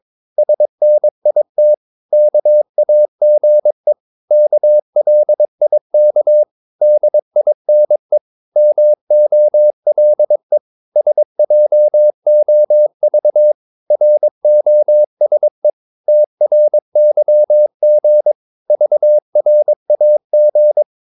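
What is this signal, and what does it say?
Morse code, heard as a keyed tone, about 22 wpm, 600 Hz. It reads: SNIT KAGE KLIK DINE MOLE SJOV ROSE TRYG VRAG
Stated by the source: Morse practice of short Danish words at 22 wpm, from CW med Gnister.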